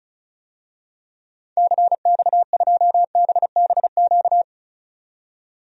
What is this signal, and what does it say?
Morse code, heard as a keyed tone, about 36 wpm, 700 Hz. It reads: CX2BBQ
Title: DXCC-100 Call Signs - 6 Characters - 35wpm